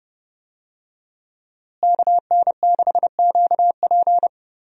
Morse code, heard as a keyed tone, about 30 wpm, 700 Hz. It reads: KN6QP
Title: USA Call Signs - 5 Characters - Repeated 3x - 30wpm